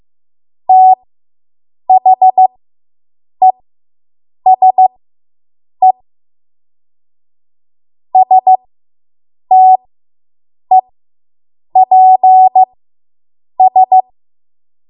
Morse code, heard as a keyed tone, about 15 wpm, spaced as for 7 wpm, 750 Hz. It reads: THESE STEPS